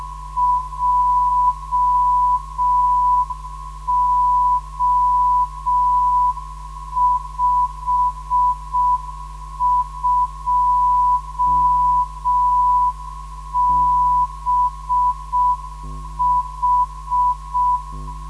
Morse code, heard as a keyed tone, about 6 wpm, 990 Hz. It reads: JO52BH
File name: ndb zvok / DI2BO 440 - 2